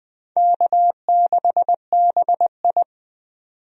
Morse code, heard as a keyed tone, about 20 wpm, 700 Hz. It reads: K6BI
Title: USA Call Signs - 4 Characters - 20wpm